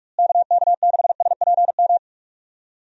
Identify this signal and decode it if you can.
Morse code, heard as a keyed tone, about 46 wpm, 700 Hz. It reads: KK6SPM
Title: USA Call Signs - 6 Characters - Repeated 2x - 45wpm